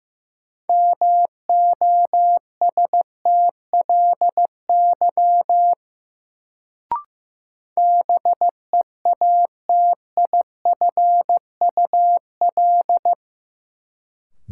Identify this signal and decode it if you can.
Morse code, heard as a keyed tone, about 15 wpm, 700 Hz. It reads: MOSTLY BEATIFUL
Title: Sentences from Top 1300 Words - New Word Drill - 15wpm